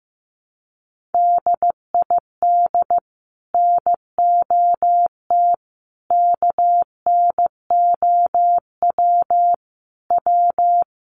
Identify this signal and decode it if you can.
Morse code, heard as a keyed tone, about 15 wpm, 700 Hz. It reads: DID NOT KNOW W
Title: Sets of 4 Words 15wpm